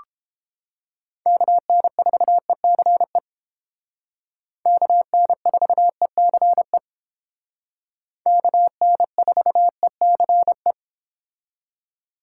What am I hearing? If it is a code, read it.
Morse code, heard as a keyed tone, about 31 wpm, 700 Hz. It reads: KN4ECE KN4ECE KN4ECE